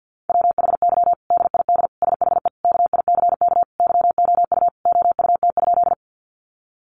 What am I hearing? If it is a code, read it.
Morse code, heard as a keyed tone, about 50 wpm, 700 Hz. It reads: WHY DID SHE KICK YOU OUT?
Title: Sentences from Top 1300 Words 50wpm